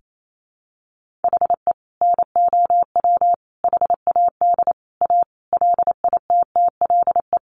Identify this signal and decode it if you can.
Morse code, heard as a keyed tone, about 28 wpm, 700 Hz. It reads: HE NOW HAD A LITTLE
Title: Sentences from Top 1500 Words 28wpm